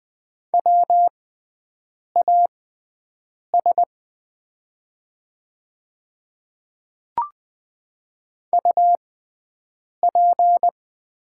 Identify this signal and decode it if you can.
Morse code, heard as a keyed tone, about 20 wpm, spaced as for 7 wpm, 700 Hz. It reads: WAS UP
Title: 6x Character Spacing - Top 100 Words - 20wpm